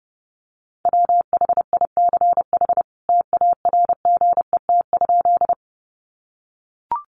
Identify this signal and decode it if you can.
Morse code, heard as a keyed tone, about 30 wpm, 700 Hz. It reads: WHICH TARGET?
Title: Sentences from Top 1200 Words 30wpm